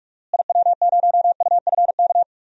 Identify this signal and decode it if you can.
Morse code, heard as a keyed tone, about 48 wpm, 700 Hz. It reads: IW0UFK
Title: DXCC-100 Call Signs - 6 Characters - Repeated 2x - 45wpm